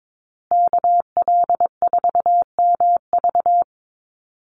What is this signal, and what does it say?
Morse code, heard as a keyed tone, about 22 wpm, 700 Hz. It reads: KL4MV
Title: USA Call Signs Repeated 2x 22wpm